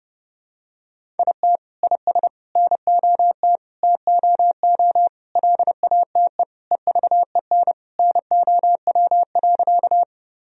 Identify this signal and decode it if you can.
Morse code, heard as a keyed tone, about 30 wpm, 700 Hz. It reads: IT IS NOT TOO LATE EVEN NOW.
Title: Sentences from Top 1000 Words - Review Part 1 - 30wpm